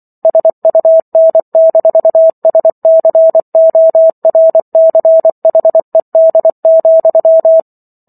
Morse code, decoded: SUN-SCORCHED,

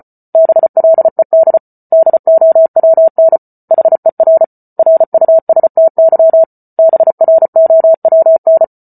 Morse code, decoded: BLED DOWN HER RUSTY BROWN

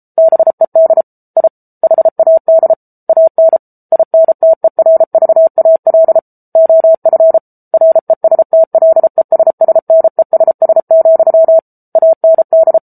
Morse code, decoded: BED I HAD AN INTERVAL OF RESTLESSNESS, AND

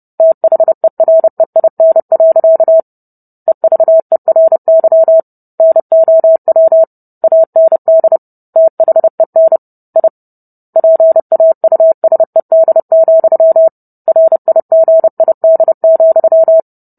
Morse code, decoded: THEREIN. EVERY NOW AND THEN I PAUSED, RIGID,

